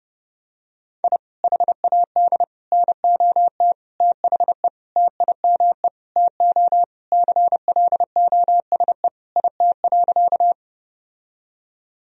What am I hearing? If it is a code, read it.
Morse code, heard as a keyed tone, about 30 wpm, 700 Hz. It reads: I HAD NOT THE TIME TO CLOSE IT.
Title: Sentences from Top 1000 Words - Review Part 1 - 30wpm